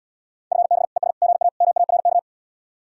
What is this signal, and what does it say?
Morse code, heard as a keyed tone, about 80 wpm, 700 Hz. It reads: FB ES GA MARK